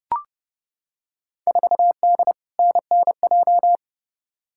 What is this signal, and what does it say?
Morse code, heard as a keyed tone, about 30 wpm, 700 Hz. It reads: 4D NNJ
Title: Field Day Exchanges Encore - 30wpm